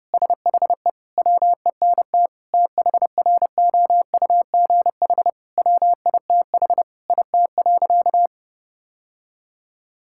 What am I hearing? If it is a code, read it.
Morse code, heard as a keyed tone, about 30 wpm, 700 Hz. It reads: SHE WENT THROUGH WITH IT.
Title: Sentences from Top 1000 Words - Review Part 6 - 30wpm